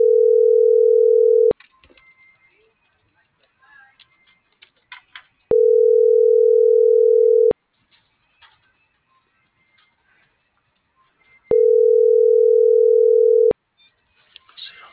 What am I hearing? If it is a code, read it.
Ringback tone (North America), 440 + 480 Hz.